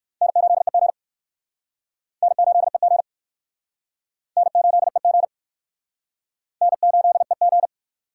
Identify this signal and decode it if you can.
Morse code, heard as a keyed tone, about 60 wpm, 700 Hz. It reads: N8EG N8EG N8EG N8EG